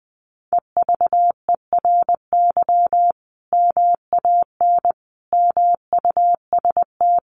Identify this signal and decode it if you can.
Morse code, heard as a keyed tone, about 19 wpm, 700 Hz. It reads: EVERY MAN MUST
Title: Sentences from Top 1400 Words 20wpm